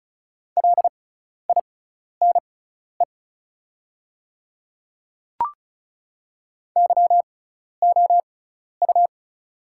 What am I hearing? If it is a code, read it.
Morse code, heard as a keyed tone, about 34 wpm, spaced as for 12 wpm, 700 Hz. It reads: LINE YOU
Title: 6x Character Spacing - Top 200 Words - 35wpm